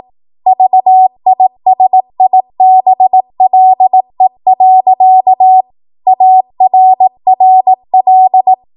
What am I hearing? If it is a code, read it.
Morse code, heard as a keyed tone, about 18 wpm, 750 Hz. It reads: VISIBLE. ARRL
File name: Morse / 18WPM